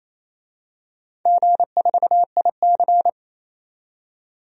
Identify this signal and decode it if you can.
Morse code, heard as a keyed tone, about 27 wpm, 700 Hz. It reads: G4IC
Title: DXCC-100 Call Signs - 4 Characters - 28wpm